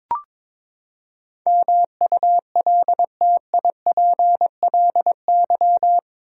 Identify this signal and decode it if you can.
Morse code, heard as a keyed tone, about 21 wpm, 700 Hz. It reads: MULTIPLY
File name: practice / 22wpm / Eight Letter Words - 22wpm